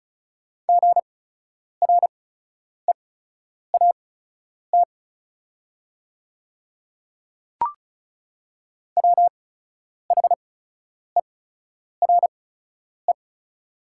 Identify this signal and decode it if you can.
Morse code, heard as a keyed tone, about 34 wpm, spaced as for 10 wpm, 700 Hz. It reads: GREAT WHERE
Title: 8x Character Spacing - Top 300 Words - 35wpm